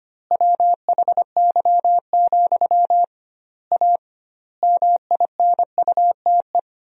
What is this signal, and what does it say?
Morse code, heard as a keyed tone, about 25 wpm, 700 Hz. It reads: WHY, A MINUTE
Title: 2x Word-Spacing - Sentences from Top 500 Words - 25wpm